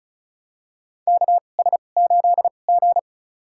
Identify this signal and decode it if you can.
Morse code, heard as a keyed tone, about 34 wpm, spaced as for 25 wpm, 690 Hz. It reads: KS8G
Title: USA Call Signs - 4 Characters - 2x Character Spacing - 35wpm